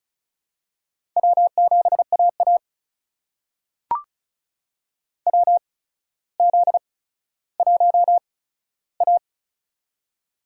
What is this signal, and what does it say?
Morse code, heard as a keyed tone, about 36 wpm, spaced as for 15 wpm, 700 Hz. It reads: W7AA WZ1A
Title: USA Call Signs - 4 Characters - 8x Character Spacing - 35wpm